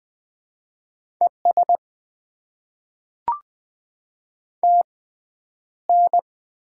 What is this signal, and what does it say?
Morse code, heard as a keyed tone, about 20 wpm, spaced as for 10 wpm, 700 Hz. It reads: ES TN
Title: Top 20 QSO Elements - 6x Character Spacing - 20wpm